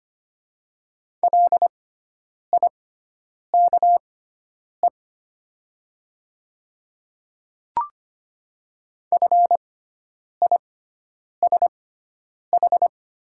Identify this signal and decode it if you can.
Morse code, heard as a keyed tone, about 25 wpm, spaced as for 9 wpm, 690 Hz. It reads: LIKE FISH
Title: 6x Character Spacing - Top 300 Words - 25wpm